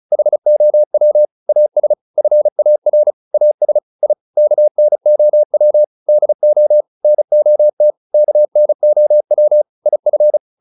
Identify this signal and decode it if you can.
Morse code, heard as a keyed tone, about 34 wpm, 600 Hz. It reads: HOW AS FAR AS I KNOW DO NOT KNOW IF
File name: WD403 35